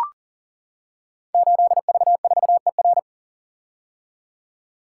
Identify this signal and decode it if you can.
Morse code, heard as a keyed tone, about 38 wpm, 700 Hz. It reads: 8V4ER
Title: Code Groups - 5 Letter:Numbers - 40wpm